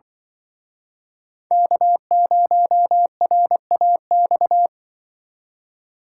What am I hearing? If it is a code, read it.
Morse code, heard as a keyed tone, about 26 wpm, 700 Hz. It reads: K0RAX